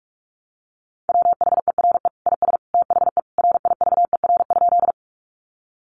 Morse code, heard as a keyed tone, about 44 wpm, 700 Hz. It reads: WHERE IS THE RIVER?
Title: Sentences from Top 300 Words 45wpm